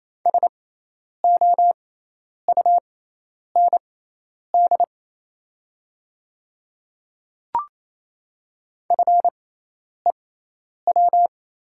Morse code, heard as a keyed tone, about 28 wpm, spaced as for 10 wpm, 700 Hz. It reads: SOUND FEW
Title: 6x Character Spacing - Top 300 Words - 28wpm